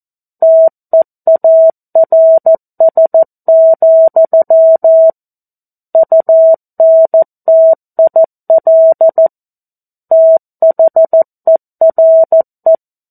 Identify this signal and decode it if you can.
Morse code, heard as a keyed tone, about 14 wpm, 650 Hz. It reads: TEARS, UNTIL THERE
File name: 02-Chapter-02 14 WPM